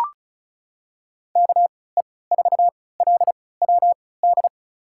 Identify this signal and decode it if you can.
Morse code, heard as a keyed tone, about 36 wpm, spaced as for 20 wpm, 700 Hz. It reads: KE4LWD
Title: USA Call Signs - 6 Characters - 3x Character Spacing - 35wpm